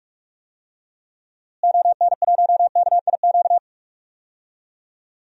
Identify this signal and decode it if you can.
Morse code, heard as a keyed tone, about 46 wpm, 700 Hz. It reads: ON1KIQ
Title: DXCC-100 Call Signs - 6 Characters - Repeated 2x - 45wpm